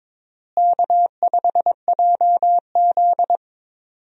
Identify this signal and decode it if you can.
Morse code, heard as a keyed tone, about 22 wpm, 700 Hz. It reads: K5JZ
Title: USA Call Signs - 4 Characters - Repeated 2x - 22wpm